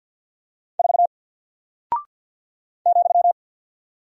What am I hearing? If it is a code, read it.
Morse code, heard as a keyed tone, about 50 wpm, 700 Hz. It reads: 4 ,
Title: Single Character (Letter:Number:Punctuation:Prosign) 50wpm